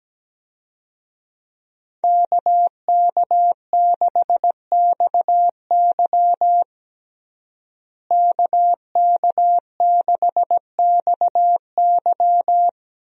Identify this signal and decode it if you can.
Morse code, heard as a keyed tone, about 17 wpm, 700 Hz. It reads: KK6XY KK6XY
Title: USA Call Signs - 5 Characters - Repeated 2x - 17wpm